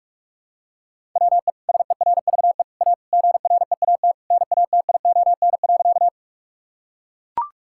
Morse code, heard as a keyed tone, about 41 wpm, 700 Hz. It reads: WE SERVE A GREAT NATION.
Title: Sentences from Top 1000 Words - Review Part 6 - 45wpm